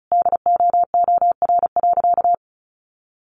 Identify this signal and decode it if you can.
Morse code, heard as a keyed tone, about 34 wpm, 700 Hz. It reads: DOOR.
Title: Sentences from Top 1400 Words 35wpm